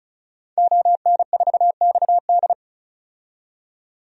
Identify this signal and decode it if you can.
Morse code, heard as a keyed tone, about 36 wpm, 700 Hz. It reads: ON4XD
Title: DXCC-100 Call Signs - 5 Characters - 35wpm